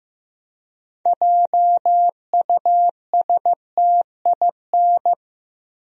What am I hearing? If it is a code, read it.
Morse code, heard as a keyed tone, about 15 wpm, 700 Hz. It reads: JUSTIN
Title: USA Common Names - 15wpm